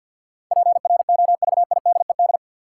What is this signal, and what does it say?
Morse code, heard as a keyed tone, about 52 wpm, 700 Hz. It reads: PROVIDED
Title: Sentences from Top 1500 Words - New Word Drill - 50wpm